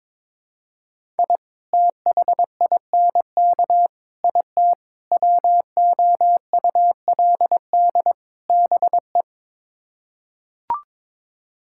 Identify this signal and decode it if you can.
Morse code, heard as a keyed tone, about 21 wpm, 700 Hz. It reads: I THINK IT WOULD BE
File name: practice / 22wpm / Build A Phrase - 5-Word Phrases - 22wpm